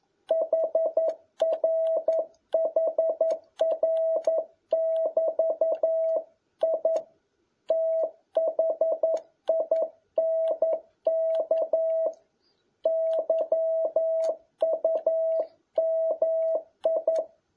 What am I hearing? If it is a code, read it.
Morse code, heard as a keyed tone, about 11 wpm, 650 Hz. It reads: HRHR<BT>I THINK YUMI
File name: jh8cbh / 2021cw